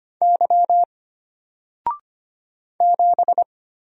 Morse code, heard as a keyed tone, about 25 wpm, 700 Hz. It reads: Y 7
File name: Single-Letter-Number-25wpm